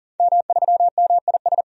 Morse code, heard as a keyed tone, about 40 wpm, 700 Hz. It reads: M3MIS
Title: DXCC-100 Call Signs - 5 Characters - 40wpm